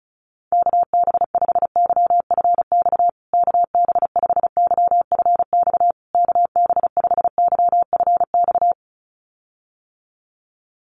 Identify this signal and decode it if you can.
Morse code, heard as a keyed tone, about 34 wpm, 700 Hz. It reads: KB5YFX KB5YFX KB5YFX
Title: USA Call Signs Repeated 3x 35wpm